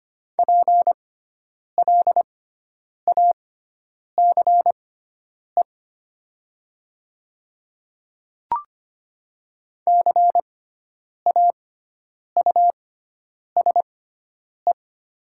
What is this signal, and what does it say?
Morse code, heard as a keyed tone, about 25 wpm, spaced as for 9 wpm, 700 Hz. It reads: PLACE CAUSE
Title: 6x Character Spacing - Top 300 Words - 25wpm